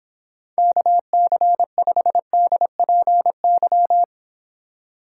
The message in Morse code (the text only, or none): KC5DPY